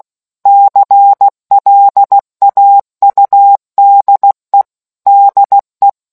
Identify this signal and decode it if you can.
Morse code, heard as a keyed tone, about 16 wpm, 800 Hz. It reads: CLAUDE DE